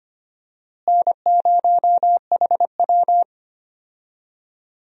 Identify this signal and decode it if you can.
Morse code, heard as a keyed tone, about 25 wpm, 700 Hz. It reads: N0HW
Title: USA Call Signs - 4 Characters - Repeated 2x - 25wpm